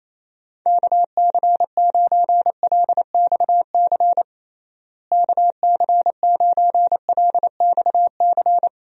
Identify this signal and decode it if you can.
Morse code, heard as a keyed tone, about 27 wpm, 700 Hz. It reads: KC9LXC KC9LXC
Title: USA Call Signs - 6 Characters - Repeated 2x - 28wpm